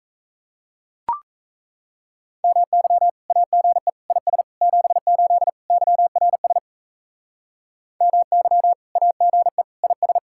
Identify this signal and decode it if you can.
Morse code, heard as a keyed tone, about 34 wpm, 700 Hz. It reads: MY AGE IS 78 YRS MY AGE IS
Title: Speed-Racing Common QSO Phrases - 28wpm